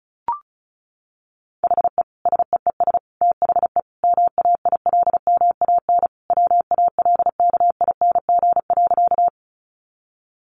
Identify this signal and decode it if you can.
Morse code, heard as a keyed tone, about 34 wpm, 700 Hz. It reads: HE SEES THE MAILMAN WALKING.
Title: Easy Phrases 35wpm